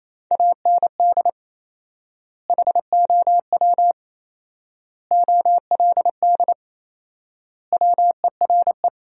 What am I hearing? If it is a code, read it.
Morse code, heard as a keyed tone, about 28 wpm, 700 Hz. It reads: AND HOW OLD WERE
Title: 4x Word-Spacing - Sentences from Top 200 Words - 28wpm